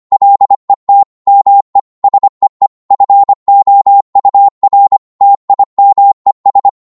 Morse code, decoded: LET ME SEE FOUR TIMES